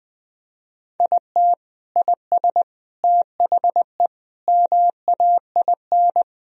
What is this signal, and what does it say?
Morse code, heard as a keyed tone, about 21 wpm, 700 Hz. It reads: IT IS THE MAIN
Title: Sentences from Top 1000 Words - Review Part 2 - 20wpm